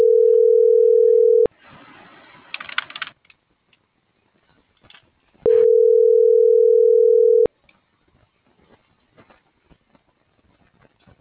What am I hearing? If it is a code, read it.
Ringback tone (North America), 440 + 480 Hz.